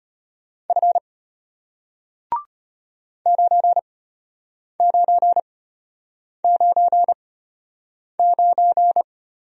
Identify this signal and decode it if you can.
Morse code, heard as a keyed tone, about 33 wpm, spaced as for 8 wpm, 700 Hz. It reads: F 9999